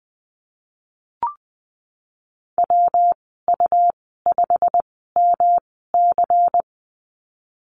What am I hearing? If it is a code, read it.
Morse code, heard as a keyed tone, about 20 wpm, spaced as for 15 wpm, 700 Hz. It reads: WU5MC
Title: 161 Lesson 26g - Letter g - Call Signs 20wpm